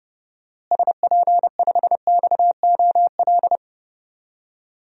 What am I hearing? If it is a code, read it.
Morse code, heard as a keyed tone, about 30 wpm, 700 Hz. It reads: SP5XOL
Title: DXCC-100 Call Signs - 6 Characters - Repeated 3x - 30wpm